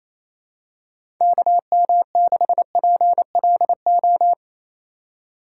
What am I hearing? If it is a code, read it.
Morse code, heard as a keyed tone, about 28 wpm, 700 Hz. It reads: KM6PLO
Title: USA Call Signs - 6 Characters - 2x Character Spacing - 28wpm